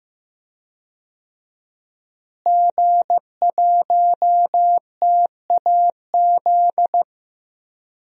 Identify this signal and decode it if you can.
Morse code, heard as a keyed tone, about 15 wpm, 700 Hz. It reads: G1TAZ
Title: DXCC-100 Call Signs - 5 Characters - Repeated 2x - 15wpm